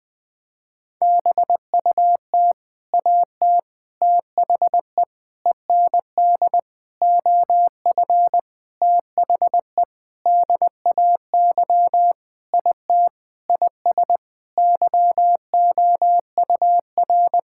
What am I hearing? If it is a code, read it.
Morse code, heard as a keyed tone, about 20 wpm, 700 Hz. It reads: BUT AT THE END OF THE DAY IT IS YOUR
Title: Sentences with 8-Word Phrases - Spoken Beginning - 20wpm